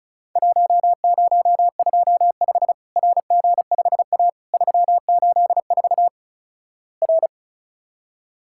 Morse code, heard as a keyed tone, about 36 wpm, 700 Hz. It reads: 1025 RG5A 384 R
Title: Worked All Europe DX Contest - QTC Exchange - 35wpm